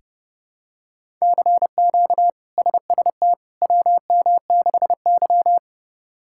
Morse code, heard as a keyed tone, about 30 wpm, 700 Hz. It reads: CQ SST WM6Y
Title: K1USN SST QSO Practice - 30wpm